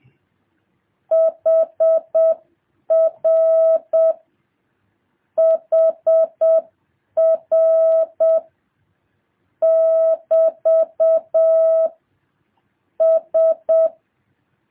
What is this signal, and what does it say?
Morse code, heard as a keyed tone, about 7 wpm, 1300 Hz. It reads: HR HR <BT> S
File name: jh8cbh / 2023cw